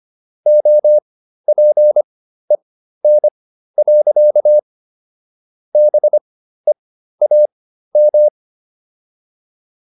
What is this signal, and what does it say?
Morse code, heard as a keyed tone, about 25 wpm, spaced as for 13 wpm, 600 Hz. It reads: OPEN. BEAM